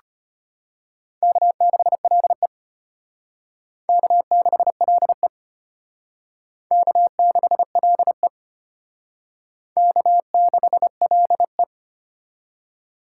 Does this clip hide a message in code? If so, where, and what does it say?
Yes: Morse code from 1.2 to 11.6 s, heard as a keyed tone, about 34 wpm, 700 Hz. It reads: K6LE K6LE K6LE K6LE